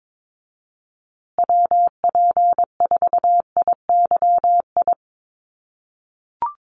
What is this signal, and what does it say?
Morse code, heard as a keyed tone, about 22 wpm, 700 Hz. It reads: WP4IYI
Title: USA Call Signs Repeated 3x 22wpm